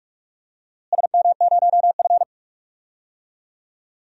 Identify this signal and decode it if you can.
Morse code, heard as a keyed tone, about 42 wpm, 700 Hz. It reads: SM0F
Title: DXCC-100 Call Signs - 4 Characters - Repeated 2x - 45wpm